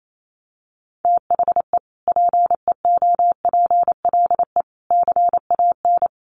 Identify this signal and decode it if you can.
Morse code, heard as a keyed tone, about 28 wpm, 700 Hz. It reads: THE PEOPLE CAN
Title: Sentences from Top 1400 Words 28wpm